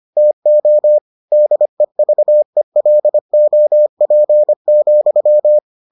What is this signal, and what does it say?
Morse code, heard as a keyed tone, about 25 wpm, 600 Hz. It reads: TO DEVELOP,